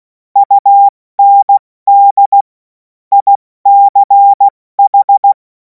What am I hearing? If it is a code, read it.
Morse code, heard as a keyed tone, about 16 wpm, 800 Hz. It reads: UND ICH
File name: Morsekurs / Witze / Witz 03. Februar 2026 70 F80